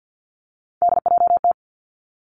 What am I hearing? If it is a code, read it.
Morse code, heard as a keyed tone, about 50 wpm, 700 Hz. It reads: DJT
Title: Three Challenging Letter-Number Combinations 50wpm